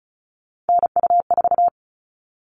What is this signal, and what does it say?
Morse code, heard as a keyed tone, about 34 wpm, 700 Hz. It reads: NU4
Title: All Cut Numbers - 3-Digit 35wpm